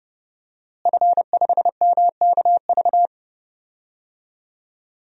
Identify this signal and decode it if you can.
Morse code, heard as a keyed tone, about 30 wpm, 700 Hz. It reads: F5MKV
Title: DXCC-100 Call Signs - 5 Characters - Repeated 3x - 30wpm